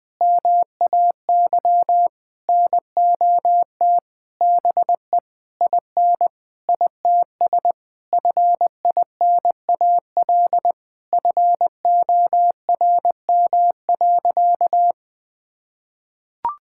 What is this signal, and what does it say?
Morse code, heard as a keyed tone, about 21 wpm, 700 Hz. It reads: MAY NOT BE IN ITS FINAL FORM.
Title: Sentences with 7-Word Phrases - 20wpm